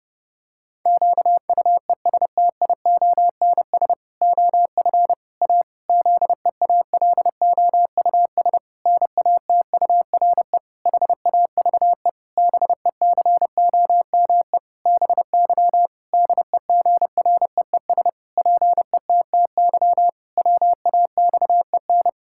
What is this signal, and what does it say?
Morse code, heard as a keyed tone, about 30 wpm, 700 Hz. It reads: QUESTIONS OF A ZEALOUS NATURE HAVE BECOME BY DEGREES PETTY WAXEN
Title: Pangram Challenge v2 - 30wpm